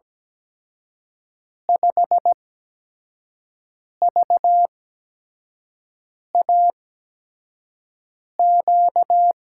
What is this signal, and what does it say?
Morse code, heard as a keyed tone, about 17 wpm, spaced as for 5 wpm, 700 Hz. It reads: 5VAQ
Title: USA Call Signs - 8x Character Spacing - 17wpm